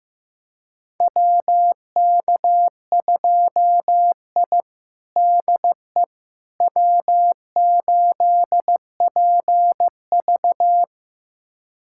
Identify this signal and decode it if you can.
Morse code, heard as a keyed tone, about 16 wpm, 700 Hz. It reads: WK2I DE W8PV
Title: USA Callsign DE Callsign - 4 Characters - 15wpm